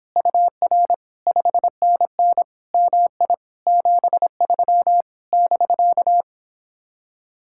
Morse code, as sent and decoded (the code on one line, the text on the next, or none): ..- .-. / ..... -. -. / -- .. / --... ...-- / -...-.-
UR 5NN MI 73 <BK>